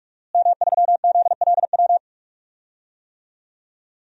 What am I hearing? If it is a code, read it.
Morse code, heard as a keyed tone, about 46 wpm, 700 Hz. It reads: M3ZLW
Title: DXCC-100 Call Signs - 5 Characters - Repeated 2x - 45wpm